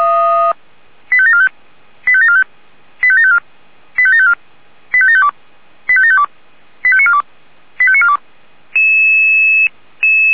Five-tone selective call (ZVEI), heard as quick runs of five tones, tones of about 70 ms. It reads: 85745 85745 85754 85754 85752 85752 85942 85942